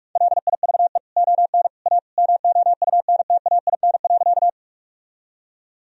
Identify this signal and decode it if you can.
Morse code, heard as a keyed tone, about 46 wpm, 700 Hz. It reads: LIVE ON A MOUNTAIN.
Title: Sentences from Top 1000 Words - Review Part 1 - 45wpm